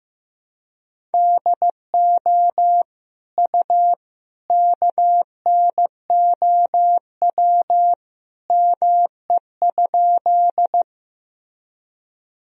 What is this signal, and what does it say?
Morse code, heard as a keyed tone, about 15 wpm, 700 Hz. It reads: DO U KNOW ME?
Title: Sentences from Top 200 Words and Abbreviations 15wpm